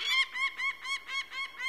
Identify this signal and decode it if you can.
There is no code.